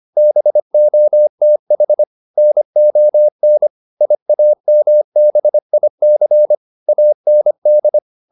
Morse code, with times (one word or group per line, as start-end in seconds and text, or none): 0.2-2.0 BOTH
2.4-3.7 NON
4.0-6.5 IAMBIC
6.9-8.0 AND